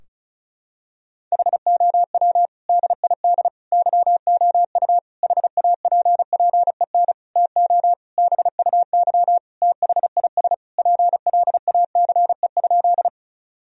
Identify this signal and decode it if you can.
Morse code, heard as a keyed tone, about 36 wpm, 700 Hz. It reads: HOW DID YOU HAPPEN TO BUY THIS PLACE?
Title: Sentences from Top 1000 Words - Review Part 4 - 35wpm